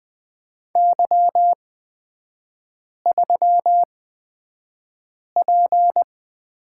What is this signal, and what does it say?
Morse code, heard as a keyed tone, about 20 wpm, spaced as for 5 wpm, 690 Hz. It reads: Y3P
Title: 229 Lesson 37e - Letter x - Three Characters 20wpm